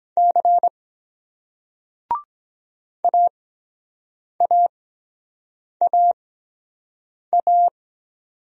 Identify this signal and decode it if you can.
Morse code, heard as a keyed tone, about 23 wpm, 700 Hz. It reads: C A A A A